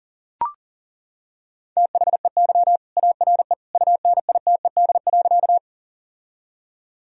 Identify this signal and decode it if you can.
Morse code, heard as a keyed tone, about 40 wpm, 700 Hz. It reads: THEY ARE UNITED.
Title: Sentences from Top 1000 Words - Review Part 4 - 40wpm